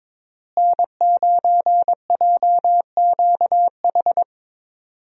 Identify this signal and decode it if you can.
Morse code, heard as a keyed tone, about 22 wpm, 700 Hz. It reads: N9JQH
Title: USA Call Signs - 5 Characters - Repeated 2x - 22wpm